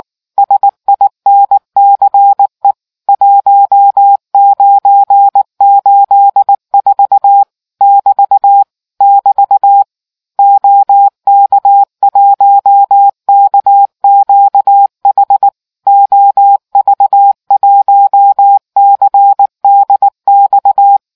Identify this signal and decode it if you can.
Morse code, heard as a keyed tone, about 19 wpm, 800 Hz. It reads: SINCE 1984 <BT> <BT> OK1KQH OV1CDX